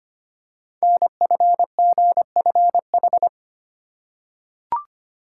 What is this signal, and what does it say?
Morse code, heard as a keyed tone, about 25 wpm, 700 Hz. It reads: NFGFH
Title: Code Groups - 5 Letters - 25wpm